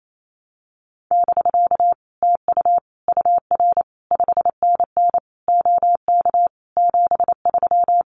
Morse code, heard as a keyed tone, about 28 wpm, 700 Hz. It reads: <BK> TU UR 5NN OK 73
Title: Common QSO Phrases Encore 28wpm